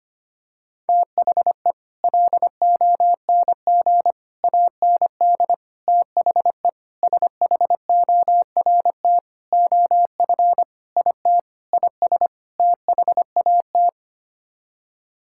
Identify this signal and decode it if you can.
Morse code, heard as a keyed tone, about 25 wpm, 700 Hz. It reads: THE LONG AND THE SHORT OF IT IS THAT
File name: Sets of 9 Words - 25wpm